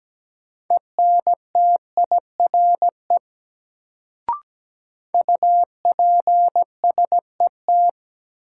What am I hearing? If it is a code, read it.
Morse code, heard as a keyed tone, about 17 wpm, 700 Hz. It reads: ENTIRE UPSET
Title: Sentences from Top 800 Words - New Word Drill - 17wpm